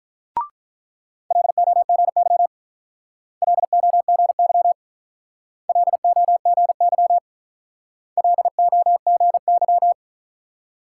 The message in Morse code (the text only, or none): LOGY LOGY LOGY LOGY